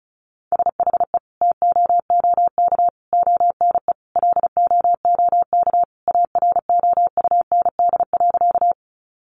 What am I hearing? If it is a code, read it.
Morse code, heard as a keyed tone, about 34 wpm, 700 Hz. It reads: SHE TOOK ONE LOOK AROUND.
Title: Sentences from Top 600 Words 35wpm